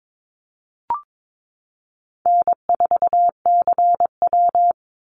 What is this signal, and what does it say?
Morse code, heard as a keyed tone, about 22 wpm, 700 Hz. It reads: N4CW
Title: Easy Call Signs - 22wpm